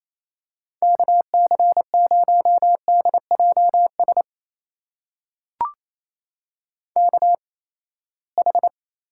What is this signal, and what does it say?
Morse code, heard as a keyed tone, about 28 wpm, spaced as for 19 wpm, 700 Hz. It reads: KC0DJS KH
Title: USA Call Signs - 8x Character Spacing - 28wpm